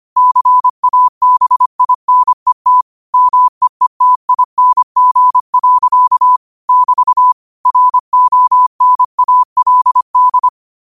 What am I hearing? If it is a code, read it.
Morse code, heard as a keyed tone, about 25 wpm, 1000 Hz. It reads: CABINET MEETING. <BT> RONALD